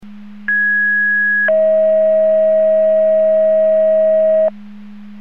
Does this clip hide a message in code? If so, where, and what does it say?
Yes: two-tone page from 0.5 to 4.5 s, heard as a one-second tone, then a three-second tone, about 1650 then 640 Hz.